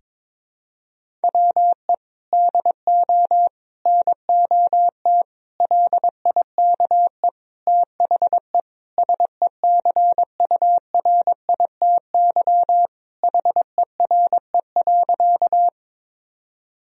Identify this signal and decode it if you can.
Morse code, heard as a keyed tone, about 22 wpm, 700 Hz. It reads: WE DO NOT LIKE THE SECURITY HERE.
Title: Sentences from Top 1000 Words - Review Part 4 - 22wpm